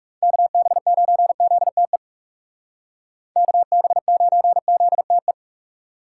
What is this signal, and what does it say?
Morse code, heard as a keyed tone, about 39 wpm, 700 Hz. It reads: KB9ZTE KB9ZTE